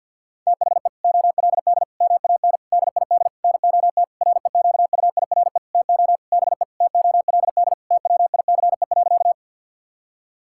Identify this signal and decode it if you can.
Morse code, heard as a keyed tone, about 50 wpm, 700 Hz. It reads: THE OLD MAN DID NOT REQUIRE TO BE TOLD TWICE.
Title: Sentences from Top 1600 Words 50wpm